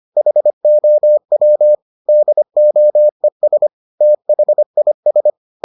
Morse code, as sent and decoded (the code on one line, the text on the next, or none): .... --- .-- / -.. --- . ... / - .... .. ...
HOW DOES THIS